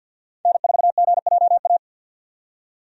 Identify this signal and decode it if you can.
Morse code, heard as a keyed tone, about 47 wpm, 700 Hz. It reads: N4GJA
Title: USA Call Signs - 5 Characters - 50wpm